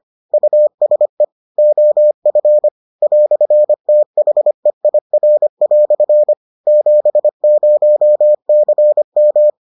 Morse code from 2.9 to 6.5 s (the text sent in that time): "THEIR"